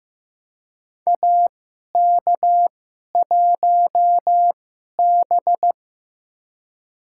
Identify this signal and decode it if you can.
Morse code, heard as a keyed tone, about 16 wpm, spaced as for 11 wpm, 700 Hz. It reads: AK1B
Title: USA Call Signs - 2x Character Spacing - 15wpm